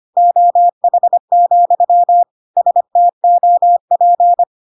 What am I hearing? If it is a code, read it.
Morse code, heard as a keyed tone, about 25 wpm, 700 Hz. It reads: OH, STOP
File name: Pinocchio / Pinocchio ch1 25 25 ik1hge